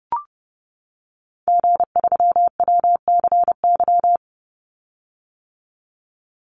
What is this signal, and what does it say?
Morse code, heard as a keyed tone, about 30 wpm, 700 Hz. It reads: G3WCY
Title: Call Signs - 30wpm